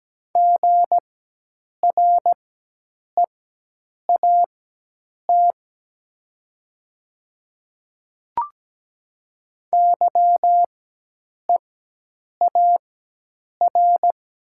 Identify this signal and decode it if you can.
Morse code, heard as a keyed tone, about 17 wpm, spaced as for 8 wpm, 700 Hz. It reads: GREAT YEAR
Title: 4x Character Spacing - Top 300 Words - 17wpm